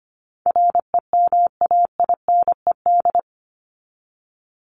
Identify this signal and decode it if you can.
Morse code, heard as a keyed tone, about 25 wpm, 700 Hz. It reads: REMAINED
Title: ED Suffix - Top 100 Words with Base Words 25wpm